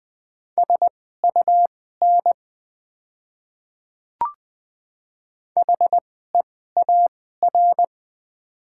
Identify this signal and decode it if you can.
Morse code, heard as a keyed tone, about 20 wpm, spaced as for 15 wpm, 700 Hz. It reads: SUN HEAR
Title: 2x Character Spacing - Top 300 Words - 20wpm